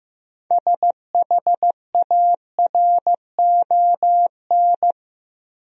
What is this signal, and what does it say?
Morse code, heard as a keyed tone, about 15 wpm, 700 Hz. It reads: SHARON
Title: USA Common Names - 15wpm